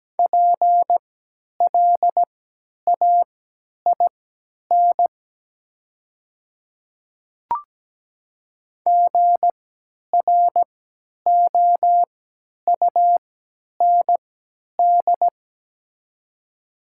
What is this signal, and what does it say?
Morse code, heard as a keyed tone, about 17 wpm, spaced as for 10 wpm, 700 Hz. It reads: PLAIN GROUND
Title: 3x Character Spacing - Top 500 Words - 17wpm